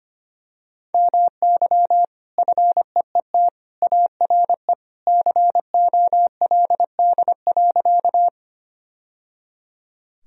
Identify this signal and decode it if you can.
Morse code, heard as a keyed tone, about 25 wpm, 700 Hz. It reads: MY FEET ARE COLD.
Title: Sentences from Top 1000 Words - Review Part 1 - 25wpm